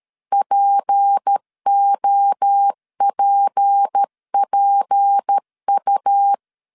Morse code, heard as a keyed tone, about 13 wpm, 800 Hz. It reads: POPPU
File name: LEZIONI / 6 PKX0 / 6 P 12wpm 2033